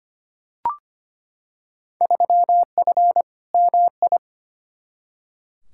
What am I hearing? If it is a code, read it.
Morse code, heard as a keyed tone, about 25 wpm, 700 Hz. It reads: 3F MI
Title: Field Day Exchanges - 25wpm